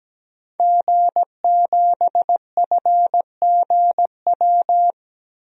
Morse code, heard as a keyed tone, about 17 wpm, 700 Hz. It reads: G7FGW
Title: DXCC-100 Call Signs - 5 Characters - Repeated 2x - 17wpm